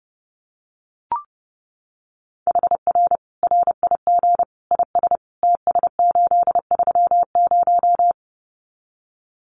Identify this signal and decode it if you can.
Morse code, heard as a keyed tone, about 30 wpm, 700 Hz. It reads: HR RIG IS TS830
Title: Common QSO Phrases 30wpm